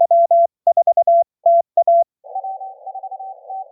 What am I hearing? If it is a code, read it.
Morse code, heard as a keyed tone, about 23 wpm, 690 Hz. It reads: W4TA